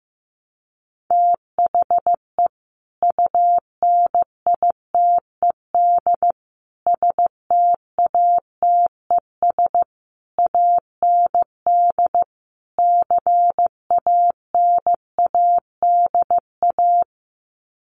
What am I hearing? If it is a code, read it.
Morse code, heard as a keyed tone, about 15 wpm, 700 Hz. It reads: THE UNITED STATES AND CANADA
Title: Sets of 5 Words 15wpm